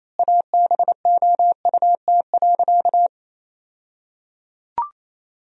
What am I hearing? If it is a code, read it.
Morse code, heard as a keyed tone, about 28 wpm, 700 Hz. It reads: ABOUT.
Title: 2x Word-Spacing - Sentences from Top 400 Words - 28wpm